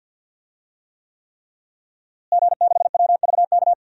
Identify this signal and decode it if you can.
Morse code, heard as a keyed tone, about 49 wpm, 700 Hz. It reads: G6WVK
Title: DXCC-100 Call Signs - 5 Characters - Repeated 3x - 50wpm